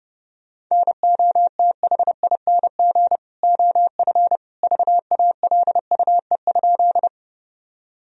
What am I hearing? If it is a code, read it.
Morse code, heard as a keyed tone, about 30 wpm, 700 Hz. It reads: NOTHING OF VALUE?
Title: Sentences from Top 1000 Words - Review Part 3 - 30wpm